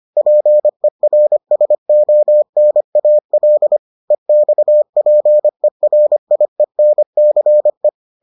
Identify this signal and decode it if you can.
Morse code, heard as a keyed tone, about 25 wpm, 600 Hz. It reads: PERSONAL EXPERIENCE